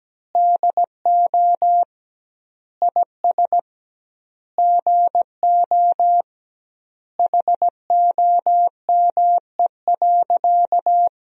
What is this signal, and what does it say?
Morse code, heard as a keyed tone, about 17 wpm, 700 Hz. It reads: DO IS GO HOME.
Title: 2x Word-Spacing - Sentences from Top 200 Words - 17wpm